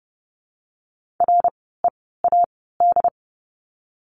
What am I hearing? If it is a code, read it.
Morse code, heard as a keyed tone, about 30 wpm, spaced as for 17 wpm, 700 Hz. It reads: READ
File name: Lesson 17f - Letter u - Words 30wpm